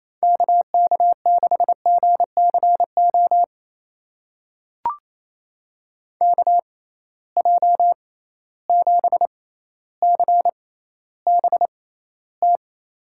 Morse code, heard as a keyed tone, about 28 wpm, spaced as for 14 wpm, 700 Hz. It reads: KK6GCO KJ7CBT
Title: USA Call Signs - 6x Character Spacing - 28wpm